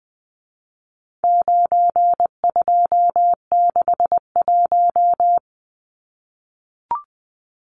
Morse code, heard as a keyed tone, about 20 wpm, 700 Hz. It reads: 9261
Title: Numbers - 4-Digit 20wpm